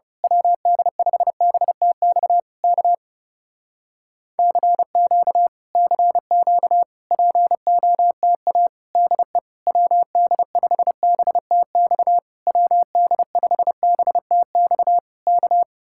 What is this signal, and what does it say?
Morse code, heard as a keyed tone, about 29 wpm, 700 Hz. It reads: WD5BTX K CQ CQ POTA DE WD5BTX WD5BTX K
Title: Speed-Racing Common QSO Phrases - 30wpm